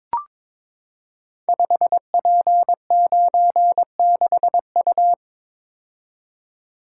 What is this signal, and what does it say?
Morse code, heard as a keyed tone, about 22 wpm, 700 Hz. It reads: 5P96U
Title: Code Groups - 5 Letter:Numbers - 22wpm